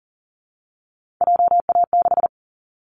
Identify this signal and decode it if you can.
Morse code, heard as a keyed tone, about 40 wpm, 700 Hz. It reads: JA6
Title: Callsign Prefixes for Top 25 DXCC - 40wpm